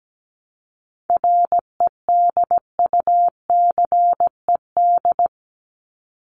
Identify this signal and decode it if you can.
Morse code, heard as a keyed tone, about 17 wpm, 700 Hz. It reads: REDUCED
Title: ED Suffix - Top 100 Words with Base Words 17wpm